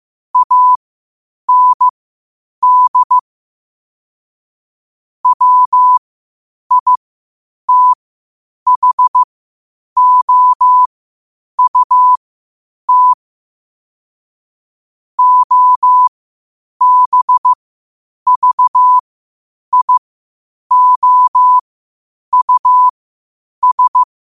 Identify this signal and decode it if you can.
Morse code, heard as a keyed tone, about 15 wpm, spaced as for 8 wpm, 1000 Hz. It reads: AND WITHOUT OBVIOUS
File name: qotdfiles / 8wpm